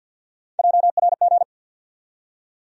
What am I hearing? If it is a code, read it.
Morse code, heard as a keyed tone, about 52 wpm, 700 Hz. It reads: JRG
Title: USA Call Sign Postfixes - 3 Letters 50wpm